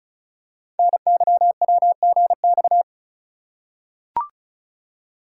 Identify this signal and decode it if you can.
Morse code, heard as a keyed tone, about 36 wpm, 690 Hz. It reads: NYWGX
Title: Code Groups - 5 Letter:Numbers - 35wpm